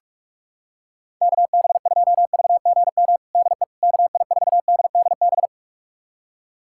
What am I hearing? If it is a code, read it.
Morse code, heard as a keyed tone, about 41 wpm, 700 Hz. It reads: KB2VGM DE KI4DDB
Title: USA Callsign DE Callsign - 6 Characters - 45wpm